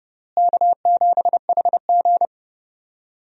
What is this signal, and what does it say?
Morse code, heard as a keyed tone, about 30 wpm, 700 Hz. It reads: K7HG
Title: USA Call Signs - 4 Characters - Repeated 2x - 30wpm